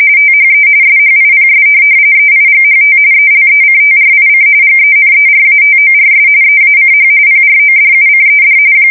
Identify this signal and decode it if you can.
RTTY, about 45.45 baud. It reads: THE SHIPS EIGHT ELECTRIC GENERATORS EACH PRODUCED
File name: fsk170 / USS-New-Jersey